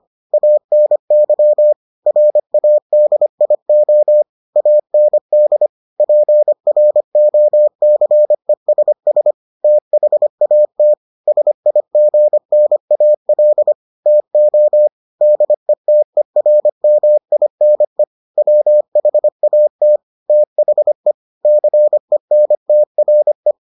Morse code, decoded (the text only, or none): ANY RADIO AND PROCESS THAT SIGNAL TO DETERMINE WHAT THE CENTRE